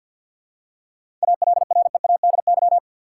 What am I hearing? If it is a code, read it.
Morse code, heard as a keyed tone, about 50 wpm, 700 Hz. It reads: ALREADY
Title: Seven Letter Words - 50wpm